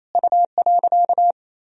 Morse code, heard as a keyed tone, about 27 wpm, 700 Hz. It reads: U.
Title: Sentences from Top 500 Words and Abbreviations 28wpm